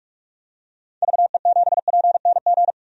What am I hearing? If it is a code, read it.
Morse code, heard as a keyed tone, about 46 wpm, 700 Hz. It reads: VE7PNG
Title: DXCC-100 Call Signs - 6 Characters - 45wpm